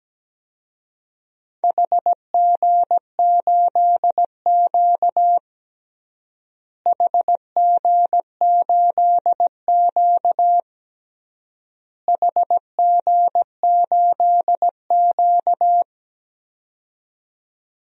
Morse code, heard as a keyed tone, about 17 wpm, 700 Hz. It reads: HG8Q HG8Q HG8Q
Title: DXCC-100 Call Signs - 4 Characters - Repeated 3x - 17wpm